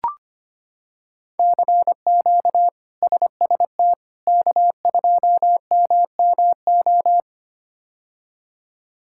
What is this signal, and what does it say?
Morse code, heard as a keyed tone, about 25 wpm, 700 Hz. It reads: CQ SST K2MMO
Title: K1USN SST QSO Practice - 25wpm